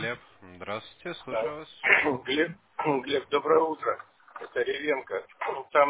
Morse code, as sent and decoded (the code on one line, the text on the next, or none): none